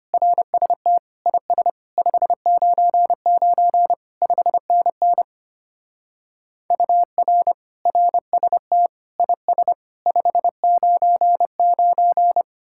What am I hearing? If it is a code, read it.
Morse code, heard as a keyed tone, about 33 wpm, 700 Hz. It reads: RST IS 599 5NN UR RST IS 599